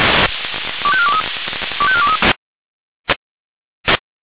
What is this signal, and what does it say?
Five-tone selective call (ZVEI), heard as quick runs of five tones, tones of about 70 ms. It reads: 25423 25423